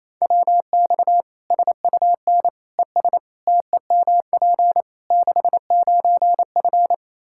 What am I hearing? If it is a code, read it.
Morse code, heard as a keyed tone, about 28 wpm, 700 Hz. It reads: WX SUN ES TEMP 69F
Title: K1USN SST QSO Practice 28wpm